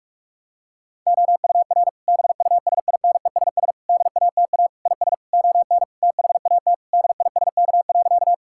Morse code, heard as a keyed tone, about 42 wpm, 700 Hz. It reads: OUR BUSINESS DATA IS ON THAT DISK.
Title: Sentences from Top 1000 Words - Review Part 6 - 45wpm